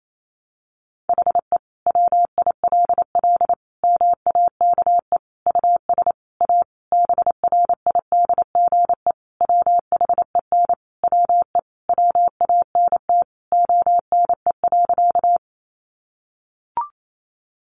Morse code, heard as a keyed tone, about 28 wpm, 700 Hz. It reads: HE WILL MAKE US A BRIDGE WHEN WE WANT ONE.
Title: Sentences from Top 1200 Words 28wpm